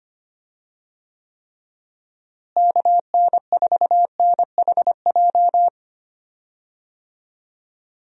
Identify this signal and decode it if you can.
Morse code, heard as a keyed tone, about 25 wpm, 700 Hz. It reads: KN4NHJ
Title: USA Call Signs - 6 Characters - Repeated 2x - 25wpm